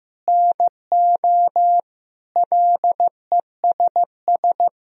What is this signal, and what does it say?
Morse code, heard as a keyed tone, about 16 wpm, 700 Hz. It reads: NO LESS
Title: Sentences from Top 1000 Words - Review Part 2 - 15wpm